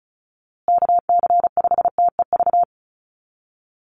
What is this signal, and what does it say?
Morse code, heard as a keyed tone, about 34 wpm, 700 Hz. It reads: KC5TEV
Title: USA Call Signs Repeated 2x 35wpm